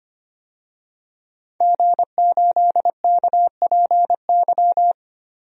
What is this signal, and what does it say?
Morse code, heard as a keyed tone, about 25 wpm, 700 Hz. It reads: G8KPY
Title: DXCC-100 Call Signs - 5 Characters - Repeated 2x - 25wpm